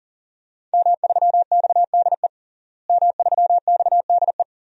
Morse code, heard as a keyed tone, about 40 wpm, 700 Hz. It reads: M3XDE M3XDE
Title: DXCC-100 Call Signs - 5 Characters - Repeated 2x - 40wpm